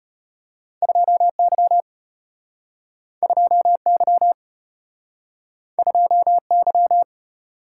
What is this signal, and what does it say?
Morse code, heard as a keyed tone, about 34 wpm, 700 Hz. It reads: 2Y 2Y 2Y